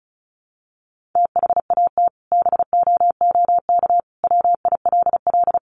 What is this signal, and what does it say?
Morse code, heard as a keyed tone, about 34 wpm, 700 Hz. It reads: THAT BOOK WILL
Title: Sentences from Top 1500 Words 35wpm